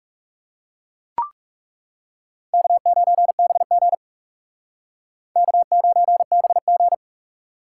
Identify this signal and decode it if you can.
Morse code, heard as a keyed tone, about 40 wpm, 700 Hz. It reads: K9BG K9BG